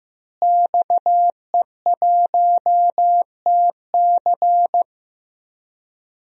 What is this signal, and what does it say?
Morse code, heard as a keyed tone, about 15 wpm, 700 Hz. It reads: XE1TC
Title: DXCC-100 Call Signs - 5 Characters - Repeated 3x - 15wpm